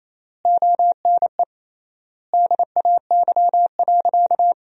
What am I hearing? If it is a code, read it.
Morse code, heard as a keyed tone, about 28 wpm, 700 Hz. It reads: ONE DAY.
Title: 3x Word-Spacing - Sentences from Top 500 Words - 28wpm